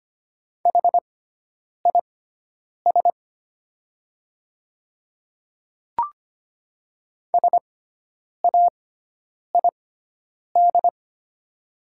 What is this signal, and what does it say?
Morse code, heard as a keyed tone, about 25 wpm, spaced as for 9 wpm, 700 Hz. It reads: HIS SAID
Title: 6x Character Spacing - Top 100 Words - 25wpm